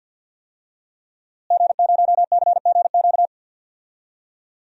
Morse code, heard as a keyed tone, about 51 wpm, 700 Hz. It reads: G0CGQ
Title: DXCC-100 Call Signs - 5 Characters - Repeated 3x - 50wpm